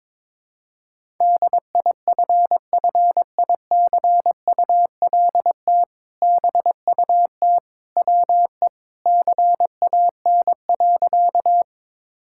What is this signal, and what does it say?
Morse code, heard as a keyed tone, about 22 wpm, 700 Hz. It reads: DIFFICULT BUT WE CAN.
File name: practice / 22wpm / Sentences with 2-Word Phrases - 22wpm